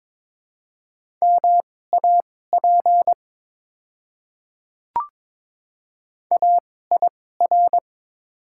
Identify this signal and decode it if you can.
Morse code, heard as a keyed tone, about 22 wpm, spaced as for 16 wpm, 700 Hz. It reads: MAP AIR
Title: 2x Character Spacing - Top 500 Words - 22wpm